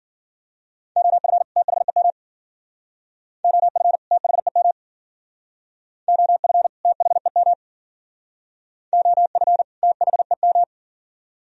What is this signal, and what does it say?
Morse code, heard as a keyed tone, about 51 wpm, 690 Hz. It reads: OF THEM OF THEM OF THEM OF THEM